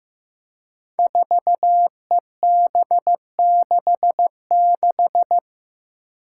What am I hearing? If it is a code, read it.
Morse code, heard as a keyed tone, about 15 wpm, 700 Hz. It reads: 4EB66
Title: Five Challenging Letter-Number Combinations 15wpm